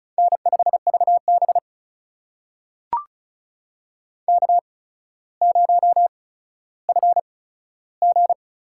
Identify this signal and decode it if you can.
Morse code, heard as a keyed tone, about 36 wpm, spaced as for 15 wpm, 700 Hz. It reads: N5VB K0FG
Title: USA Call Signs - 4 Characters - 8x Character Spacing - 35wpm